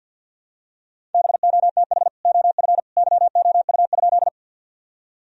Morse code, heard as a keyed tone, about 50 wpm, 700 Hz. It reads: BOTH OF YOU?
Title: Sentences from Top 1000 Words - Review Part 1 - 50wpm